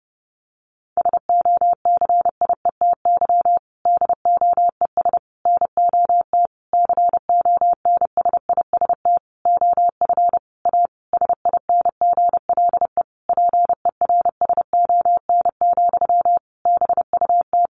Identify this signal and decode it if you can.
Morse code, heard as a keyed tone, about 30 wpm, 700 Hz. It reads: SOCIETY DOES NOT CONSIST OF A SINGLE PERSON, BUT
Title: Sentences from Top 1400 Words 30wpm